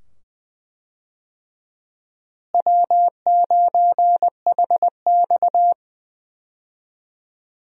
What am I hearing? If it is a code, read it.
Morse code, heard as a keyed tone, about 20 wpm, 700 Hz. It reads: W9HX
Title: USA Call Signs - 4 Characters - Repeated 2x - 20wpm